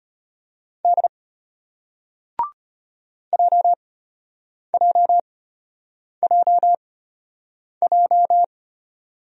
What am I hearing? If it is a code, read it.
Morse code, heard as a keyed tone, about 35 wpm, spaced as for 8 wpm, 700 Hz. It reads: D JJJJ